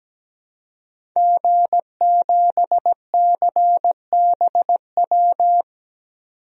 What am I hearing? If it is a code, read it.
Morse code, heard as a keyed tone, about 17 wpm, 700 Hz. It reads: G7CBW
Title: DXCC-100 Call Signs - 5 Characters - 17wpm